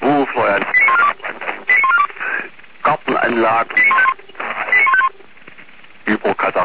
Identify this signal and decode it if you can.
Five-tone selective call (ZVEI), heard as quick runs of five tones, tones of about 70 ms, with speech. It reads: 89142 89142 89151 89151